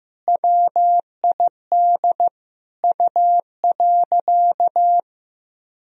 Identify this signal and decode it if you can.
Morse code, heard as a keyed tone, about 15 wpm, 700 Hz. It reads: WID U.
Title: Sentences from Top 500 Words and Abbreviations 15wpm